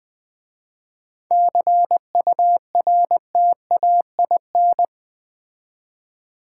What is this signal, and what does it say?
Morse code, heard as a keyed tone, about 20 wpm, 700 Hz. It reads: CURTAIN
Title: 7 Letter Words - First 3 Letters Given - 20wpm